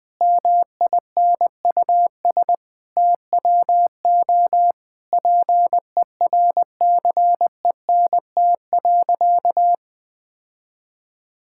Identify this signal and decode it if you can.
Morse code, heard as a keyed tone, about 20 wpm, 700 Hz. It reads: MINUS TWO PERCENT.
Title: Sentences with 8-Word Phrases - 20wpm